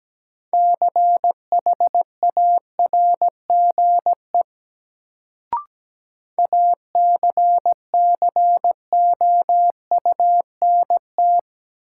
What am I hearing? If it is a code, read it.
Morse code, heard as a keyed tone, about 17 wpm, 700 Hz. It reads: CHARGE ACCOUNT
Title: Sentences from Top 900 Words - New Word Drill - 17wpm